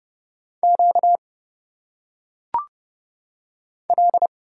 Morse code, heard as a keyed tone, about 29 wpm, 700 Hz. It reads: Q L
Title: Single Letters - Phonetic Pronunciation 30wpm